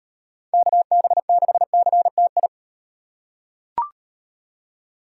Morse code, heard as a keyed tone, about 38 wpm, 700 Hz. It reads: KB6CTI